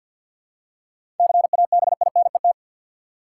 Morse code, heard as a keyed tone, about 52 wpm, 700 Hz. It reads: CABINET
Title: Seven Letter Words - 50wpm